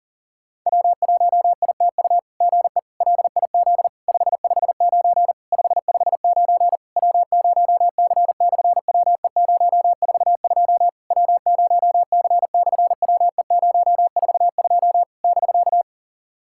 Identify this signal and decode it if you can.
Morse code, heard as a keyed tone, about 44 wpm, 700 Hz. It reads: W1ITU GE LIZ 559 559 W0C/WE042 W0C/WE042 <BK>
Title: SOTA QSO Practice 40wpm